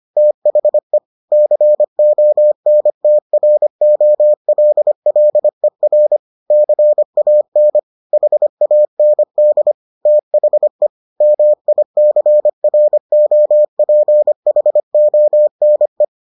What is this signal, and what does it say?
Morse code, heard as a keyed tone, about 25 wpm, 600 Hz. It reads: THE CONTROLLER CAN HAND THE MICROPHONE